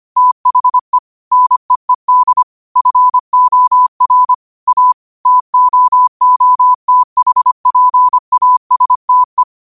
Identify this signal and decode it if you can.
Morse code, heard as a keyed tone, about 25 wpm, 1000 Hz. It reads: THE NEED FOR A TOOTHPASTE
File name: 25wpm